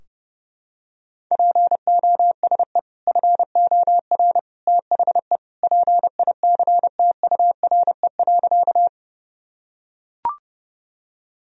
Morse code, heard as a keyed tone, about 30 wpm, 700 Hz. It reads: POSE FOR THE PICTURE.
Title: Sentences from Top 1000 Words - Review Part 2 - 30wpm